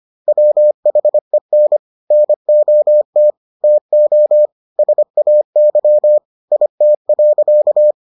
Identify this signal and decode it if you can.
Morse code, heard as a keyed tone, about 25 wpm, 600 Hz. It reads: WHEN NOT TO SAY IT.